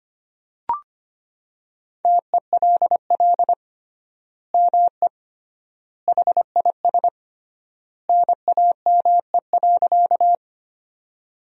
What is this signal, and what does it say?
Morse code, heard as a keyed tone, about 25 wpm, 700 Hz. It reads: TELL ME HIS NAME.
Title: 3x Word-Spacing - Sentences from Top 200 Words - 25wpm